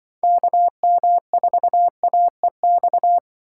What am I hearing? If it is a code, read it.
Morse code, heard as a keyed tone, about 25 wpm, 700 Hz. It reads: KM4AEX